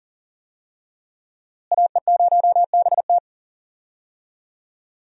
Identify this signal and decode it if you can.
Morse code, heard as a keyed tone, about 39 wpm, 700 Hz. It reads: AE0BT
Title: USA Call Signs - 5 Characters - Repeated 2x - 40wpm